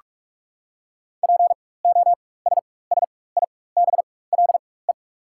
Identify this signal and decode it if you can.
Morse code, heard as a keyed tone, about 43 wpm, spaced as for 21 wpm, 700 Hz. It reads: POSSIBLE
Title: 4x Character Spacing - Top 500 Words - 45wpm